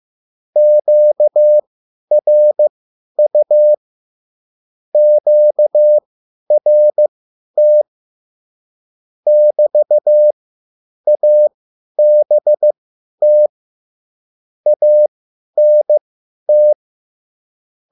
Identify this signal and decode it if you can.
Morse code, heard as a keyed tone, about 15 wpm, spaced as for 10 wpm, 600 Hz. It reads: QRU QRT <BT> ABT ANT
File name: QSL / CW lesson